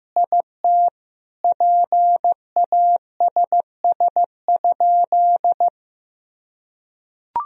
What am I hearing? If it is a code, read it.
Morse code, heard as a keyed tone, about 15 wpm, 700 Hz. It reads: IT PASS?
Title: Sentences from Top 400 Words and Abbreviations 15wpm